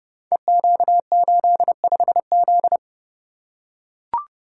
Morse code, heard as a keyed tone, about 30 wpm, 700 Hz. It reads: EQ85Z
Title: Code Groups - 5 Letter:Numbers - 30wpm